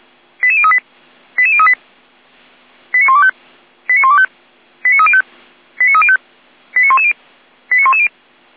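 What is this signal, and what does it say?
Five-tone selective call (ZVEI), heard as quick runs of five tones, tones of about 70 ms. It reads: 88038 88038 89125 89125 89305 89305 89110 89110
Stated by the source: ZVEI five-tone selective call 89277, received by SDR.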